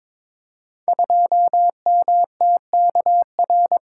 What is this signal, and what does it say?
Morse code, heard as a keyed tone, about 23 wpm, 700 Hz. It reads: 2MTKR
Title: Code Groups - 5 Letter:Numbers - 22wpm